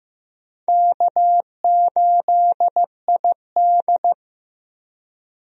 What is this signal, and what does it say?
Morse code, heard as a keyed tone, about 15 wpm, 700 Hz. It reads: K8ID
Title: USA Call Signs - 4 Characters - Repeated 2x - 15wpm